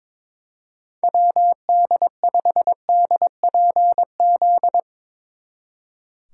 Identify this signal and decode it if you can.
Morse code, heard as a keyed tone, about 23 wpm, 700 Hz. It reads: WD5DPZ
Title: Speed Racing - USA 2x3 Call Signs 22wpm